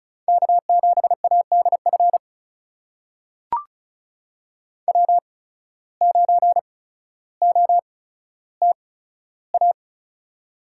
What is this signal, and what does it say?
Morse code, heard as a keyed tone, about 34 wpm, spaced as for 15 wpm, 700 Hz. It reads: K7ADF W9OTA
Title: USA Call Signs - 5 Characters - 8x Character Spacing - 35wpm